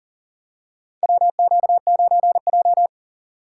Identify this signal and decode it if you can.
Morse code, heard as a keyed tone, about 40 wpm, 700 Hz. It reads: WQ9J
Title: USA Call Signs - 4 Characters - 2x Character Spacing - 40wpm